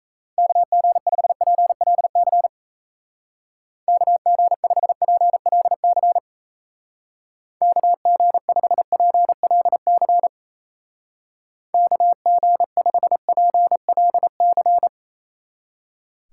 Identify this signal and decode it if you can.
Morse code, heard as a keyed tone, about 34 wpm, 700 Hz. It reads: KG5PLC KG5PLC KG5PLC KG5PLC
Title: Speed Racing - USA 2x3 Call Signs 28wpm